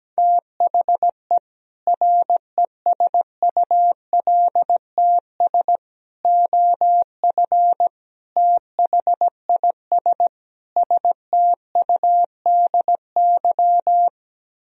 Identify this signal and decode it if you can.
Morse code, heard as a keyed tone, about 17 wpm, 700 Hz. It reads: THE RESULTS OF THIS STUDY
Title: Speed-Racing - Sets of 5 Words 17wpm